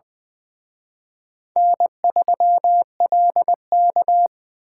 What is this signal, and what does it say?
Morse code, heard as a keyed tone, about 24 wpm, 700 Hz. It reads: N3LK
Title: Speed Racing - USA 1x2 Call Signs 20wpm